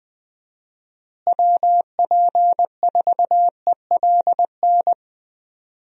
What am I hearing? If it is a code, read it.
Morse code, heard as a keyed tone, about 20 wpm, 700 Hz. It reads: WP4ELN
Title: USA Call Signs - 6 Characters - Repeated 3x - 20wpm